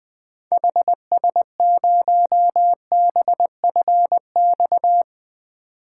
Morse code, heard as a keyed tone, about 21 wpm, 700 Hz. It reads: HS0BFX
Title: 2023 POTA Activator Call Signs - 20wpm